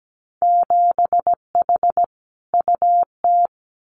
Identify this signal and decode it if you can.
Morse code, heard as a keyed tone, about 17 wpm, 700 Hz. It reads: 7H UT
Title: Winter Field Day Exchanges Encore 17wpm